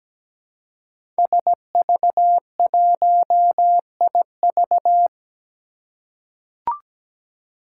Morse code, heard as a keyed tone, about 17 wpm, 700 Hz. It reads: SV1IV
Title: Five Challenging Letter-Number Combinations 17wpm